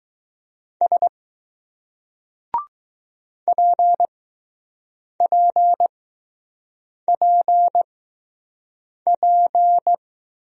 Morse code, heard as a keyed tone, about 20 wpm, spaced as for 6 wpm, 700 Hz. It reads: S PPPP